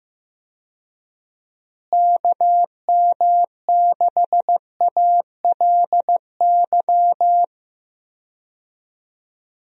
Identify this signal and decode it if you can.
Morse code, heard as a keyed tone, about 15 wpm, 700 Hz. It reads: KM6ALY
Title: USA Call Signs - 6 Characters - Repeated 2x - 15wpm